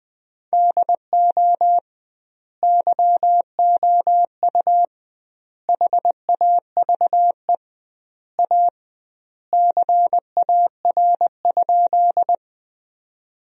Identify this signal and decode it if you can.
Morse code, heard as a keyed tone, about 20 wpm, 700 Hz. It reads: DO YOU HAVE A CAR?
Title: 2x Word-Spacing - Sentences from Top 300 Words - 20wpm